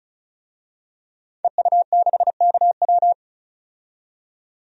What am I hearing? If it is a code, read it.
Morse code, heard as a keyed tone, about 34 wpm, 700 Hz. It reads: EU6KW
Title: DXCC-100 Call Signs - 5 Characters - 35wpm